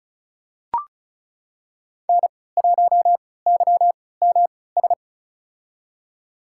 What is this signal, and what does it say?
Morse code, heard as a keyed tone, about 36 wpm, spaced as for 20 wpm, 690 Hz. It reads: N1YMS
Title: USA Call Signs - 5 Characters - 3x Character Spacing - 35wpm